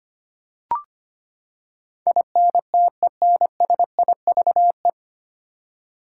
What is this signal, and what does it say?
Morse code, heard as a keyed tone, about 25 wpm, 700 Hz. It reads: INTENSIVE
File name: Nine Letter Words - 25wpm